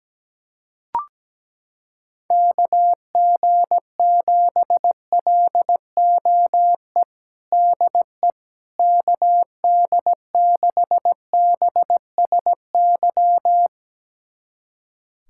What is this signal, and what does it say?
Morse code, heard as a keyed tone, about 17 wpm, 700 Hz. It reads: KG7LOE DE KD6BSY
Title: USA Callsign DE Callsign - 6 Characters - 17wpm